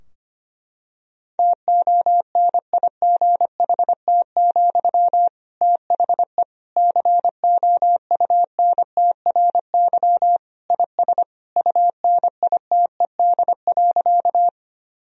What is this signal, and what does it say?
Morse code, heard as a keyed tone, about 25 wpm, 700 Hz. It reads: TONIGHT, THE COUNTRY IS UNITED.
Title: Sentences from Top 1000 Words - Review Part 4 - 25wpm